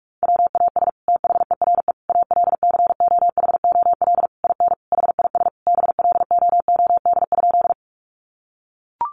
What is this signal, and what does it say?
Morse code, heard as a keyed tone, about 44 wpm, 700 Hz. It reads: WAS THERE ALCOHOL IN HIS BLOOD?
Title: Sentences from Top 1400 Words 45wpm